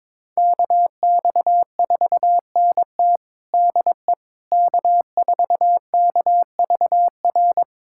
Morse code, heard as a keyed tone, about 22 wpm, 700 Hz. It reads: KX4NT DE K4KVR
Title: USA Callsign DE Callsign - 5 Characters - 22wpm